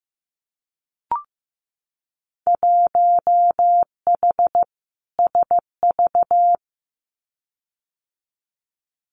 Morse code, heard as a keyed tone, about 15 wpm, 700 Hz. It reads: 1H SV
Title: Winter Field Day Exchanges 15wpm